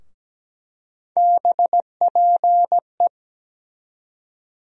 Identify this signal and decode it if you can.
Morse code, heard as a keyed tone, about 17 wpm, 700 Hz. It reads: BPE
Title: USA Call Sign Postfixes - 3 Letters 17wpm